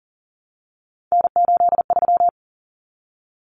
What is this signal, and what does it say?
Morse code, heard as a keyed tone, about 40 wpm, 700 Hz. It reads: N83
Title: Common Cut Numbers - 3-Digit 40wpm